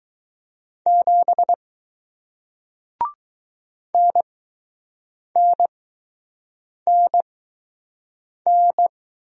Morse code, heard as a keyed tone, about 20 wpm, spaced as for 6 wpm, 700 Hz. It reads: 7 NNNN